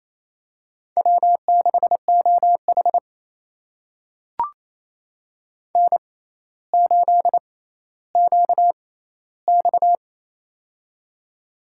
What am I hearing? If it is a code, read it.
Morse code, heard as a keyed tone, about 28 wpm, spaced as for 14 wpm, 700 Hz. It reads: W6OH N8QX
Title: USA Call Signs - 4 Characters - 6x Character Spacing - 28wpm